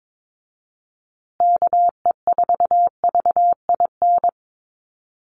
Morse code, heard as a keyed tone, about 22 wpm, 700 Hz. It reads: KE4VIN
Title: USA Call Signs Repeated 3x 22wpm